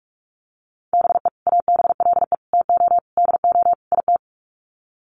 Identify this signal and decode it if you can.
Morse code, heard as a keyed tone, about 44 wpm, 700 Hz. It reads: BE ABLE TO DO IT
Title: Sets of 5 Words Encore2 45wpm